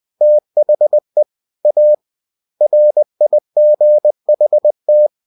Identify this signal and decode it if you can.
Morse code, heard as a keyed tone, about 20 wpm, 600 Hz. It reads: THE A RIGHT